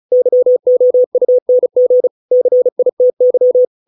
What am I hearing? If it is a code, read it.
Morse code, heard as a keyed tone, about 34 wpm, 500 Hz. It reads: YOUNG CITY